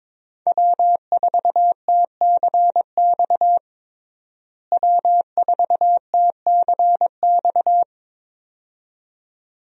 Morse code, heard as a keyed tone, about 22 wpm, 700 Hz. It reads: W4TCX W4TCX
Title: USA Call Signs - 5 Characters - Repeated 2x - 22wpm